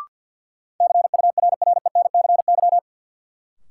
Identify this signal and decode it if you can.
Morse code, heard as a keyed tone, about 52 wpm, 700 Hz. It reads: CURRENCY
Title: Sentences from Top 1500 Words - New Word Drill - 50wpm